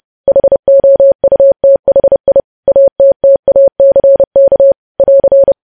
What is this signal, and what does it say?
Morse code, heard as a keyed tone, about 30 wpm, 550 Hz. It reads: HOUTHI ATTACK <AR>